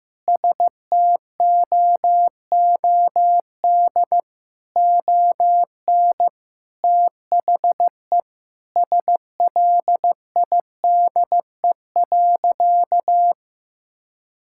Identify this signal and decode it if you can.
Morse code, heard as a keyed tone, about 15 wpm, 700 Hz. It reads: STOOD ON THE SLIDE.
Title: Sentences from Top 1600 Words 15wpm